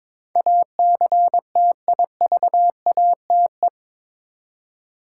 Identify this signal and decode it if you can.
Morse code, heard as a keyed tone, about 22 wpm, 700 Hz. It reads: ACTIVATE
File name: practice / 22wpm / Eight Letter Words - 22wpm